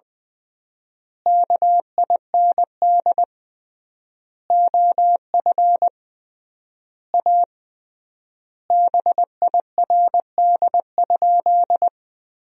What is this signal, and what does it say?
Morse code, heard as a keyed tone, about 20 wpm, 700 Hz. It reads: KIND OF A BIRD?
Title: 3x Word-Spacing - Sentences from Top 400 Words - 20wpm